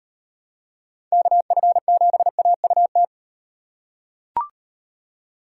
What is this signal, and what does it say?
Morse code, heard as a keyed tone, about 38 wpm, 700 Hz. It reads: KF7AUT